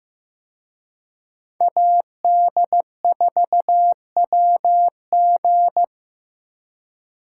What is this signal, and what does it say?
Morse code, heard as a keyed tone, about 18 wpm, 700 Hz. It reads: AD4WG
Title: Speed Racing - USA 2x2 Call Signs 15wpm